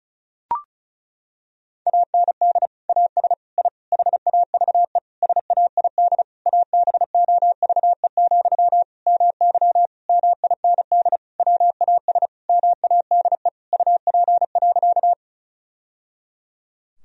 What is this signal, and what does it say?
Morse code, heard as a keyed tone, about 34 wpm, 700 Hz. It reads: AND AS I HAVE SAID ABOVE, MY MIND WAS MADE UP.
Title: Sentences from Top 1000 Words - Review Part 2 - 35wpm